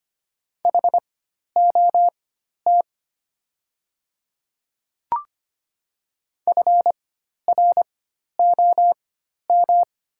Morse code, heard as a keyed tone, about 25 wpm, spaced as for 12 wpm, 700 Hz. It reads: HOT FROM
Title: 4x Character Spacing - Top 100 Words - 25wpm